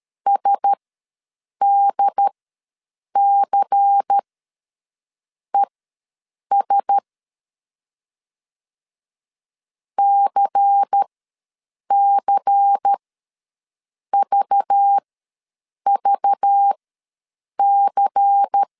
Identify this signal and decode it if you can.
Morse code, heard as a keyed tone, about 13 wpm, spaced as for 7 wpm, 800 Hz. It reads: SDCES CCVVC
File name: LEZIONI / 5 LBVC / 5 LBVC ws 12wpm 3159